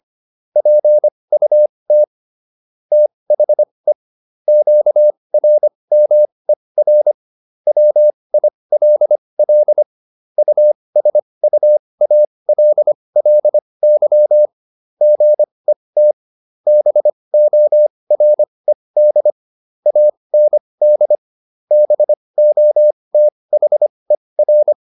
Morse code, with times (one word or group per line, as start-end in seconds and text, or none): 0.6-2.0 PUT
2.9-3.9 THE
4.5-7.1 QRMER
7.7-9.8 WILL
10.4-14.5 USUALLY
15.0-16.1 GET
16.7-19.3 BORED
19.9-21.1 AND
21.7-24.7 BOTHER